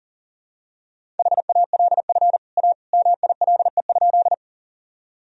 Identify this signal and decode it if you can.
Morse code, heard as a keyed tone, about 40 wpm, 700 Hz. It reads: HALF A MILE?
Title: Sentences from Top 1000 Words - Review Part 1 - 40wpm